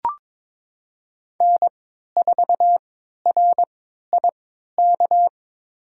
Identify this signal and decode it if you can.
Morse code, heard as a keyed tone, about 22 wpm, spaced as for 13 wpm, 700 Hz. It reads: N4RIK
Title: USA Call Signs - 5 Characters - 3x Character Spacing - 22wpm